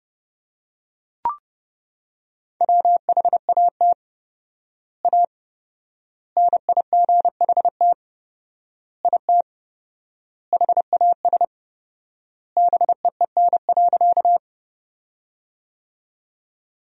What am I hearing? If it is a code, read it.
Morse code, heard as a keyed tone, about 32 wpm, 700 Hz. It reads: WHAT A NIGHT IT HAS BEEN.
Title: 4x Word-Spacing - Sentences from Top 300 Words - 30wpm